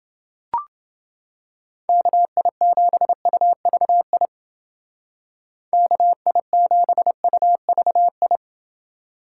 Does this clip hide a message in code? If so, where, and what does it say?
Yes: Morse code from 1.9 to 8.4 s, heard as a keyed tone, about 28 wpm, 700 Hz. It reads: KI7UVI KI7UVI